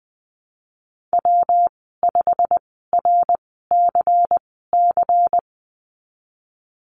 Morse code, heard as a keyed tone, about 20 wpm, spaced as for 15 wpm, 700 Hz. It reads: W5RCC
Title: 122 Lesson 20g - Number 3 - Call Signs 20wpm